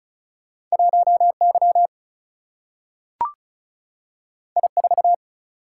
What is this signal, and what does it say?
Morse code, heard as a keyed tone, about 36 wpm, 700 Hz. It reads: 1 Y I 4
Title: Single-Letter Single-Number 35wpm